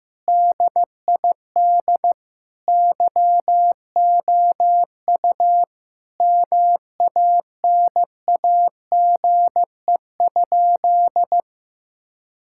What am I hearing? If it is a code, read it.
Morse code, heard as a keyed tone, about 15 wpm, 700 Hz. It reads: DID YOU MANAGE?
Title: Sentences from Top 1600 Words 15wpm